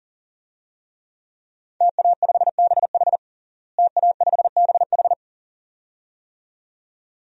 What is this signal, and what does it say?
Morse code, heard as a keyed tone, about 37 wpm, 700 Hz. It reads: TA5BH TA5BH
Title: DXCC-100 Call Signs - 5 Characters - Repeated 2x - 40wpm